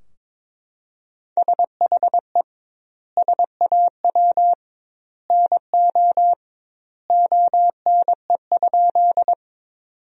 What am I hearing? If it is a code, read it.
Morse code, heard as a keyed tone, about 22 wpm, 700 Hz. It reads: SHE SAW NO ONE?
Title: 2x Word-Spacing - Sentences from Top 300 Words - 22wpm